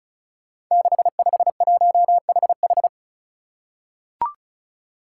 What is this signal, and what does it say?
Morse code, heard as a keyed tone, about 36 wpm, 700 Hz. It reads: 651HH
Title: Five Challenging Letter-Number Combinations 35wpm